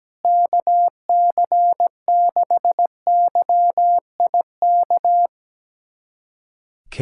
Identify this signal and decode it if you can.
Morse code, heard as a keyed tone, about 17 wpm, 700 Hz. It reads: KC6YIK
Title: USA Callsign DE Callsign 17wpm